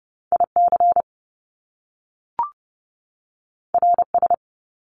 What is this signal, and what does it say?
Morse code, heard as a keyed tone, about 30 wpm, 700 Hz. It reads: IC RS
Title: Top Two Letter Combinations in Words 30wpm